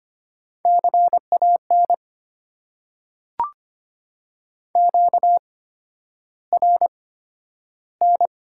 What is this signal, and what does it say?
Morse code, heard as a keyed tone, about 25 wpm, spaced as for 11 wpm, 700 Hz. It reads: CAN QRN
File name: Top 90 QSO Elements - 8x Character Spacing - 25wpm